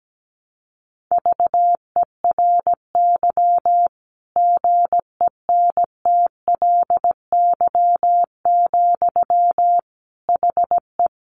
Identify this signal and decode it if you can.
Morse code, heard as a keyed tone, about 17 wpm, 700 Hz. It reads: VERY GENTLY, HE
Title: Sentences from Top 1500 Words 17wpm